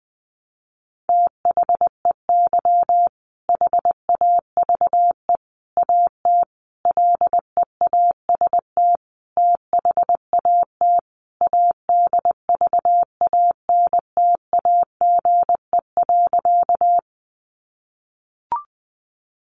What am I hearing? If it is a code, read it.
Morse code, heard as a keyed tone, about 20 wpm, 700 Hz. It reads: THEY HAVE AT LEAST THAT ADVANTAGE.
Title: Sentences from Top 900 Words 20wpm